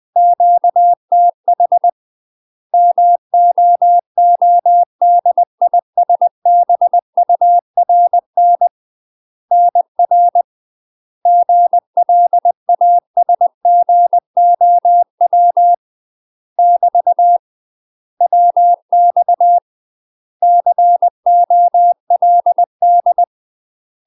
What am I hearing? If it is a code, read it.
Morse code, heard as a keyed tone, about 20 wpm, 700 Hz. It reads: QTH MOODISBURN NR GLASGOW <BT> WX COLD
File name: UK MASTER QSOs TRAINING / 20WPM / UK MASTER QSOs 0019